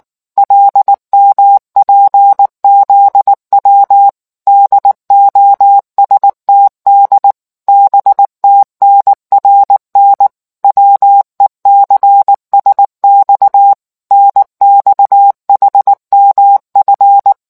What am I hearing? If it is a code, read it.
Morse code, heard as a keyed tone, about 19 wpm, 800 Hz. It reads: LMPZW DOSTD BTNRN WECSX NXHMF